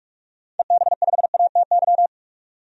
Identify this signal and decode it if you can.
Morse code, heard as a keyed tone, about 46 wpm, 700 Hz. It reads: EB5ATY
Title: DXCC-100 Call Signs - 6 Characters - 45wpm